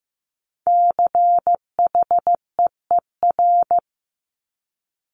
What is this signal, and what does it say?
Morse code, heard as a keyed tone, about 15 wpm, 700 Hz. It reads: CHEER
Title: Multi-letter Phonemes Repeated 3x 15wpm